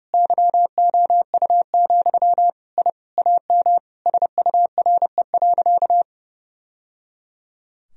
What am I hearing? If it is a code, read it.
Morse code, heard as a keyed tone, about 30 wpm, 700 Hz. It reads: YOU, I AM SURE.
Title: Sentences from Top 1000 Words - Review Part 3 - 30wpm